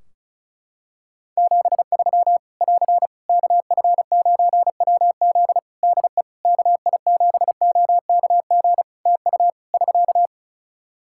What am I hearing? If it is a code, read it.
Morse code, heard as a keyed tone, about 36 wpm, 700 Hz. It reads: 73 <AR> KF9WZ DE KI7OKG TU <SK>
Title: K1USN SST QSO Practice 35wpm